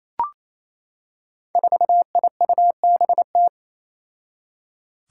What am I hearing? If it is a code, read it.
Morse code, heard as a keyed tone, about 28 wpm, 700 Hz. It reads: 4IUBT
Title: Five Challenging Letter-Number Combinations 28wpm